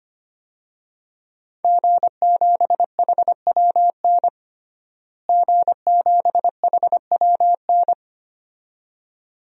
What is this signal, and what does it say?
Morse code, heard as a keyed tone, about 25 wpm, 700 Hz. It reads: G7HWN G7HWN
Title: DXCC-100 Call Signs - 5 Characters - Repeated 2x - 25wpm